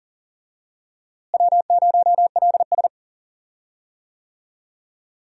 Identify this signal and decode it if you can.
Morse code, heard as a keyed tone, about 39 wpm, 700 Hz. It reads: W0LS
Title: USA Call Signs - 4 Characters - Repeated 2x - 40wpm